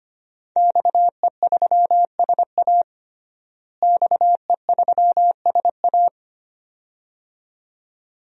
Morse code, heard as a keyed tone, about 25 wpm, 700 Hz. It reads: XE3SA XE3SA
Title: DXCC-100 Call Signs - 5 Characters - Repeated 2x - 25wpm